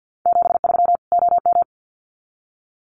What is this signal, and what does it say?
Morse code, heard as a keyed tone, about 50 wpm, 700 Hz. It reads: 73 OM
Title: Common QSO Phrases 50wpm